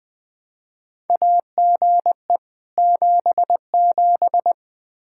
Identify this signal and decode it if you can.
Morse code, heard as a keyed tone, about 21 wpm, 700 Hz. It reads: AGE 77
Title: Practice QSO - Exchange 3 - More Info - 20wpm